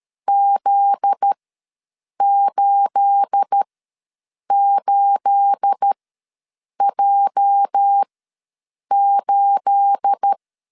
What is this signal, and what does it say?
Morse code, heard as a keyed tone, about 13 wpm, spaced as for 7 wpm, 800 Hz. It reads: Z88J8
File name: LEZIONI / 9 28 ws 12wpm 3354